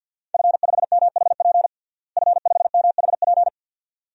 Morse code, heard as a keyed tone, about 51 wpm, 700 Hz. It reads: F5MHP F5MHP
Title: DXCC-100 Call Signs - 5 Characters - Repeated 2x - 50wpm